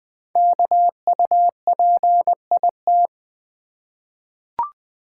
Morse code, heard as a keyed tone, about 21 wpm, 700 Hz. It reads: KUPIT
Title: Code Groups - 5 Letter:Numbers - 20wpm